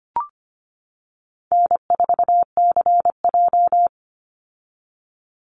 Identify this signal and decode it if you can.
Morse code, heard as a keyed tone, about 25 wpm, 700 Hz. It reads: N4CJ
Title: Easy Call Signs - 25wpm